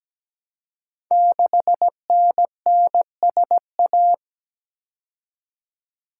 Morse code, heard as a keyed tone, about 17 wpm, 690 Hz. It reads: 6NNSA
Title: Code Groups - 5 Letter:Numbers - 17wpm